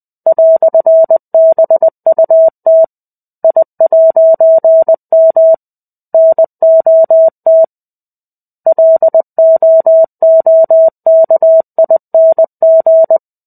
Morse code, heard as a keyed tone, about 20 wpm, 650 Hz. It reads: "BUT I'M NOT LOOKING
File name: 05-Chapter-05 20 WPM